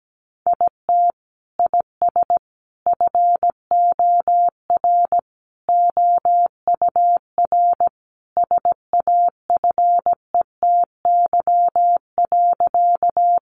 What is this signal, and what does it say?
Morse code, heard as a keyed tone, about 17 wpm, 700 Hz. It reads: IT IS FOR OUR SAFETY.
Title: Sentences from Top 700 Words 17wpm